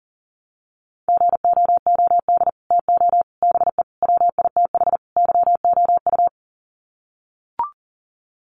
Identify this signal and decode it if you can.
Morse code, heard as a keyed tone, about 40 wpm, 700 Hz. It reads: GOOD TO BE WITH YOU
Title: Sets of 5 Words Encore2 40wpm